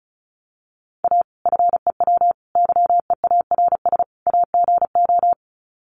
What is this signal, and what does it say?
Morse code, heard as a keyed tone, about 34 wpm, 700 Hz. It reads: A FEW YEARS AGO
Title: Sets of 4 Words 35wpm